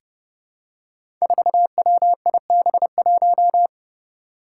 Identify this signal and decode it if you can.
Morse code, heard as a keyed tone, about 30 wpm, 700 Hz. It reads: 4WIB1
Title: Five Challenging Letter-Number Combinations 30wpm